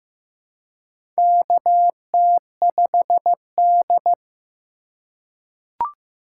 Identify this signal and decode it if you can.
Morse code, heard as a keyed tone, about 15 wpm, 700 Hz. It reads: KT5D
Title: USA Call Signs - 4 Characters - 4x Character Spacing - 15wpm